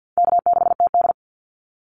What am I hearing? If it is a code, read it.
Morse code, heard as a keyed tone, about 50 wpm, 700 Hz. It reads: K6TD
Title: Easy Call Signs - 50wpm